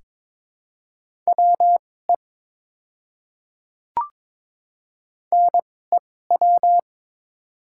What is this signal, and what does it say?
Morse code, heard as a keyed tone, about 22 wpm, spaced as for 16 wpm, 700 Hz. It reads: WE NEW
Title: 2x Character Spacing - Top 200 Words - 22wpm